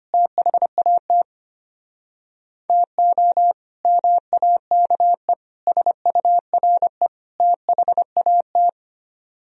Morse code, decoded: THAT TO MAKE SURE THAT

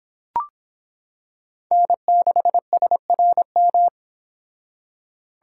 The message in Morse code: N6SRM

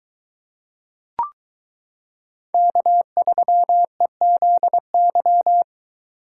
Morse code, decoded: K3EZY

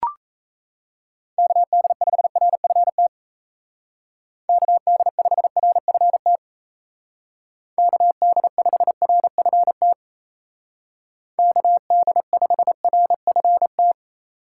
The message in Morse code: KD5RFT KD5RFT KD5RFT KD5RFT

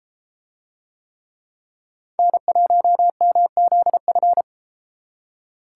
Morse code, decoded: N1MZF

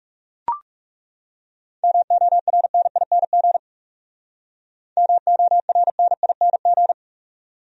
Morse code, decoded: MORNING MORNING

